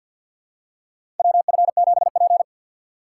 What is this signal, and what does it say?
Morse code, heard as a keyed tone, about 46 wpm, 700 Hz. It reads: WF7P
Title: USA Call Signs - 4x Character Spacing - 50wpm